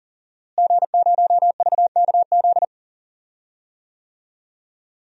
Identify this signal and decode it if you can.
Morse code, heard as a keyed tone, about 40 wpm, 700 Hz. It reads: G0VKZ
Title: DXCC-100 Call Signs - 5 Characters - Repeated 2x - 40wpm